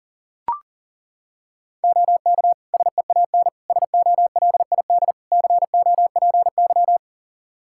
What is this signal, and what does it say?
Morse code, heard as a keyed tone, about 40 wpm, 700 Hz. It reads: OK SEAN SOLID COPY
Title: Practice QSO - Exchange 3 - More Info - 40wpm